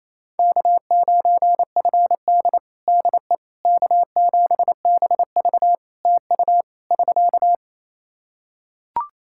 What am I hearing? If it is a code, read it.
Morse code, heard as a keyed tone, about 28 wpm, 700 Hz. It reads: K9FD DE K7BV TU <SK>
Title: Practice QSO - Exchange 4 - Ending - 28wpm